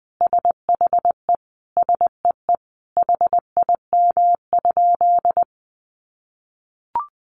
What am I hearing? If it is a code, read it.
Morse code, heard as a keyed tone, about 20 wpm, 700 Hz. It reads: SHE SEE HIM?
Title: Sentences from Top 100 Words 20wpm